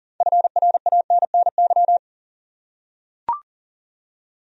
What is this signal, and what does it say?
Morse code, heard as a keyed tone, about 40 wpm, 700 Hz. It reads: FRANNY
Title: USA Common Names - 40wpm